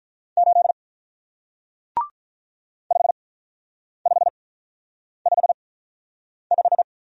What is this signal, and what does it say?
Morse code, heard as a keyed tone, about 48 wpm, spaced as for 9 wpm, 700 Hz. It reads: 8 5555